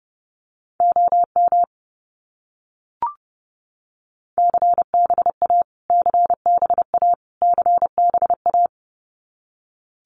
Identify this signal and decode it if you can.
Morse code, heard as a keyed tone, about 30 wpm, 700 Hz. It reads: OM CBA CBA CBA
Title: Abbreviations Repeated 3x - 30wpm